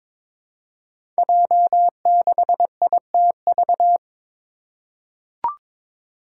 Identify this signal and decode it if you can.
Morse code, heard as a keyed tone, about 22 wpm, 700 Hz. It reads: J6ITV
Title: Five Challenging Letter-Number Combinations 22wpm